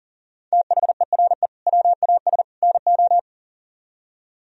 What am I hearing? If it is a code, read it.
Morse code, heard as a keyed tone, about 50 wpm, 700 Hz. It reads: THERE WAS NO
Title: Speed-Racing - Sets of 3 Words 40wpm